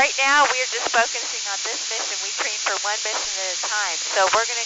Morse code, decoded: none

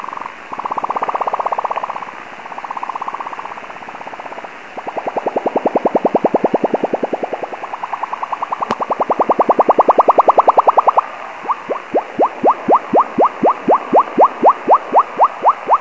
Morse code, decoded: none

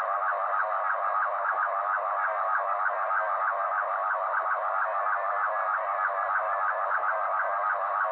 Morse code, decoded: none

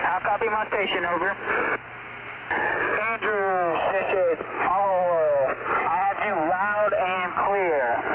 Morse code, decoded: none

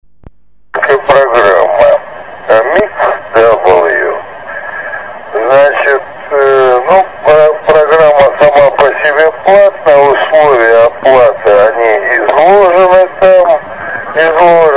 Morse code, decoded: none

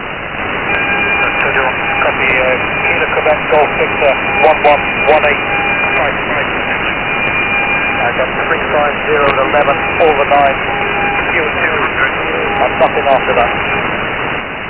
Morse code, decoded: none

